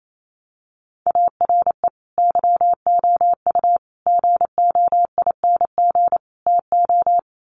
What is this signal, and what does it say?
Morse code, heard as a keyed tone, about 28 wpm, 700 Hz. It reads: ARE YOU GOING TO
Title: Sets of 4 Words 28wpm